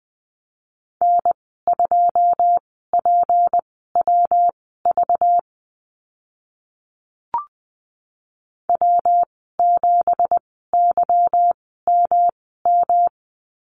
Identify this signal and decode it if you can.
Morse code, heard as a keyed tone, about 20 wpm, spaced as for 14 wpm, 700 Hz. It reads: N2PWV W7YMM
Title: 193 Lesson 31g - Letter v - Call Signs 20wpm